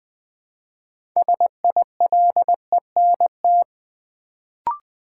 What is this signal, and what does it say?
Morse code, heard as a keyed tone, about 20 wpm, 700 Hz. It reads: SILENT
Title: Build A Word - 2 Letters At A Time - 6-Letter Words - 20wpm